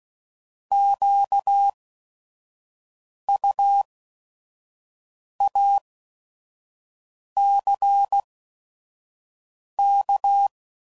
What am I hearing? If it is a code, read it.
Morse code, heard as a keyed tone, about 16 wpm, spaced as for 5 wpm, 800 Hz. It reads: QUACK
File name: Prefixes / By First Letter / Qs